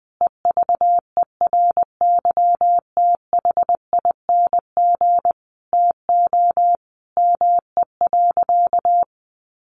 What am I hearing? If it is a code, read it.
Morse code, heard as a keyed tone, about 20 wpm, 700 Hz. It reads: EVERYTHING TO ME.
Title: Sentences from Top 700 Words 20wpm